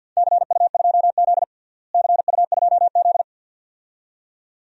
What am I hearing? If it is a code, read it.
Morse code, heard as a keyed tone, about 53 wpm, 700 Hz. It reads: CU2Z CU2Z
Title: DXCC-100 Call Signs - 4 Characters - Repeated 2x - 50wpm